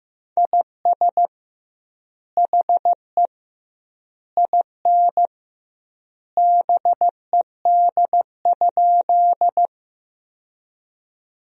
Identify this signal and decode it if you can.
Morse code, heard as a keyed tone, about 15 wpm, 700 Hz. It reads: IS HE IN BED?
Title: 2x Word-Spacing - Sentences from Top 500 Words - 15wpm